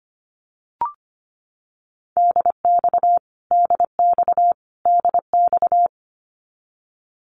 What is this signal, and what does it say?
Morse code, heard as a keyed tone, about 25 wpm, 700 Hz. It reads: DX DX DX
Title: Abbreviations Repeated 3x - 25wpm